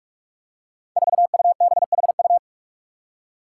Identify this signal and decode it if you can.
Morse code, heard as a keyed tone, about 44 wpm, 700 Hz. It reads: 4UBHU
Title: Five Challenging Letter-Number Combinations 45wpm